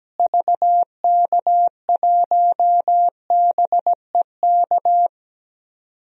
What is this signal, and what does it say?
Morse code, heard as a keyed tone, about 17 wpm, 700 Hz. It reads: VK1BEK
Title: DXCC-100 Call Signs - 6 Characters - Repeated 3x - 17wpm